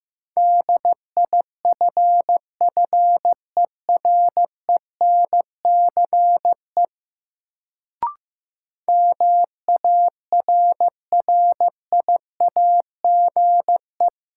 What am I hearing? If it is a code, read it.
Morse code, heard as a keyed tone, about 15 wpm, 700 Hz. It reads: DIFFERENCE MARRIAGE
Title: Sentences from Top 800 Words - New Word Drill - 15wpm